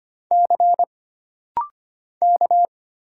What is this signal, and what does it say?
Morse code, heard as a keyed tone, about 25 wpm, 700 Hz. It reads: C K
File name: Single Character (Letter:Number:Punctuation:Prosign) - Rapid-Fire - 25wpm